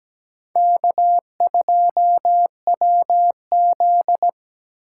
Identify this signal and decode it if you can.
Morse code, heard as a keyed tone, about 20 wpm, 700 Hz. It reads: K2WZ
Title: Speed Racing - USA 1x2 Call Signs 17wpm